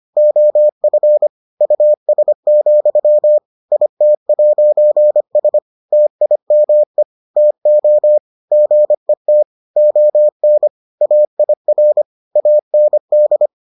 Morse code, heard as a keyed tone, about 25 wpm, 600 Hz. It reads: OF US, IT'S TIME TO GET ON AIR AND